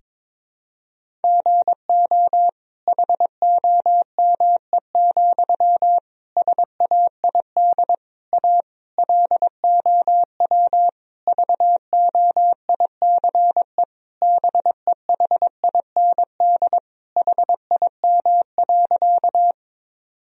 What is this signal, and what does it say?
Morse code, heard as a keyed tone, about 22 wpm, 700 Hz. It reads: GO HOME, SAID A LOW VOICE BEHIND HIM.
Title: Sentences from Top 1000 Words - Review Part 1 - 22wpm